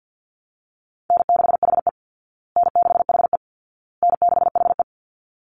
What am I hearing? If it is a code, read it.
Morse code, heard as a keyed tone, about 48 wpm, 700 Hz. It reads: N6HE N6HE N6HE
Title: Easy Call Signs Repeated 3x - 50wpm